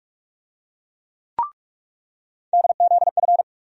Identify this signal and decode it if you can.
Morse code, heard as a keyed tone, about 42 wpm, 700 Hz. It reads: DZF
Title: USA Call Sign Postfixes - 3 Letters 45wpm